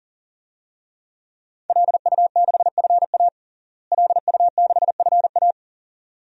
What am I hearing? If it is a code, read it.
Morse code, heard as a keyed tone, about 40 wpm, 700 Hz. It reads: LU6FA LU6FA
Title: DXCC-100 Call Signs - 5 Characters - Repeated 2x - 40wpm